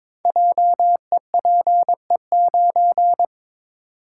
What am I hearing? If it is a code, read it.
Morse code, heard as a keyed tone, about 21 wpm, 700 Hz. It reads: JEPE9
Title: Code Groups - 5 Letter:Numbers - 22wpm